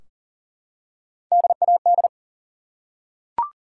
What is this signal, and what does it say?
Morse code, heard as a keyed tone, about 39 wpm, 700 Hz. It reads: DAD
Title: Three Letter Words - 40wpm